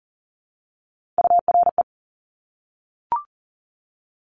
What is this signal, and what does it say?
Morse code, heard as a keyed tone, about 40 wpm, 700 Hz. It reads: URE
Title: Top Three Letter Combinations in Words 40wpm